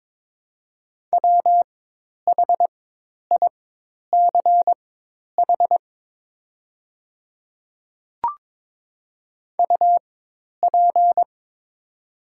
Morse code, heard as a keyed tone, about 22 wpm, spaced as for 10 wpm, 690 Hz. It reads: WHICH UP
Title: 4x Character Spacing - Top 200 Words - 22wpm